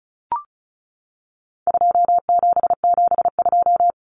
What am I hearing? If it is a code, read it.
Morse code, heard as a keyed tone, about 34 wpm, 700 Hz. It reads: 2772
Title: Common Cut Numbers - 4-Digit 35wpm